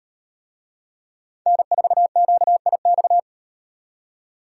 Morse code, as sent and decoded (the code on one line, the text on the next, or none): -. ....- --.- .. -..-
N4QIX